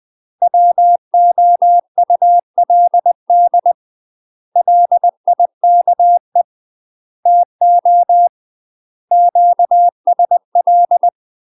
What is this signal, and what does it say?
Morse code, heard as a keyed tone, about 20 wpm, 700 Hz. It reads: WOULD LIKE TO QSL
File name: UK MASTER QSOs TRAINING / 20WPM / UK MASTER QSOs 0059